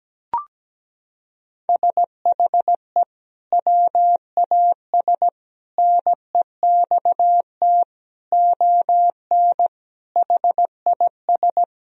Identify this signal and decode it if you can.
Morse code, heard as a keyed tone, about 17 wpm, 700 Hz. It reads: SHE WAS NEXT ON HIS
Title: Sentences from Top 1000 Words - Review Part 2 - 17wpm